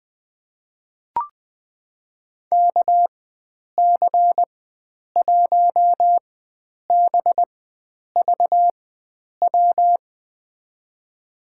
Morse code, heard as a keyed tone, about 20 wpm, spaced as for 9 wpm, 700 Hz. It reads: KC1BVW
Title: USA Call Signs - 6 Characters - 4x Character Spacing - 20wpm